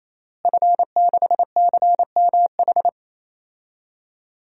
Morse code, heard as a keyed tone, about 28 wpm, 700 Hz. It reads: F6CMH
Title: DXCC-100 Call Signs - 5 Characters - Repeated 3x - 28wpm